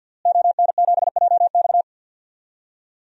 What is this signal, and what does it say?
Morse code, heard as a keyed tone, about 53 wpm, 700 Hz. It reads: ON7JX
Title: DXCC-100 Call Signs - 5 Characters - Repeated 3x - 50wpm